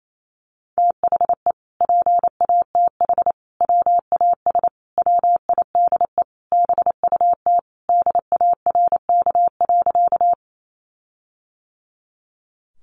Morse code, heard as a keyed tone, about 28 wpm, 700 Hz. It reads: THE PATH WAS WIDE BUT DARK.
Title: Sentences from Top 1400 Words 28wpm